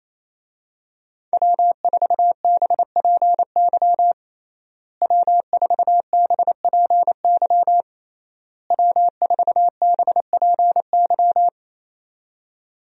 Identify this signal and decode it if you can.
Morse code, heard as a keyed tone, about 28 wpm, 700 Hz. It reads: W4BPY W4BPY W4BPY
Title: USA Call Signs - 5 Characters - Repeated 3x - 28wpm